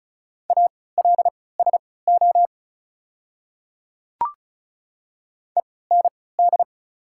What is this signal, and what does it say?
Morse code, heard as a keyed tone, about 36 wpm, spaced as for 20 wpm, 700 Hz. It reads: ALSO END
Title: 3x Character Spacing - Top 200 Words - 35wpm